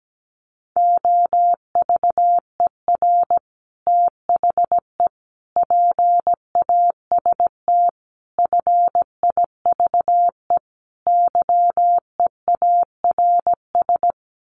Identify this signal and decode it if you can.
Morse code, heard as a keyed tone, about 17 wpm, 690 Hz. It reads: OVER THE PAST FIVE YEARS
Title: Sets of 5 Words Encore 17wpm